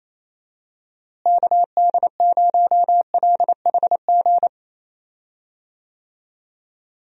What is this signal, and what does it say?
Morse code, heard as a keyed tone, about 28 wpm, 700 Hz. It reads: KD0LHG
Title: USA Call Signs - 6 Characters - Repeated 2x - 28wpm